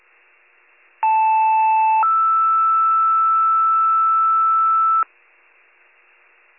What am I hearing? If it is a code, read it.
Two-tone page, heard as a one-second tone, then a three-second tone, about 880 then 1320 Hz. It is heard from 1.0 to 5.0 s.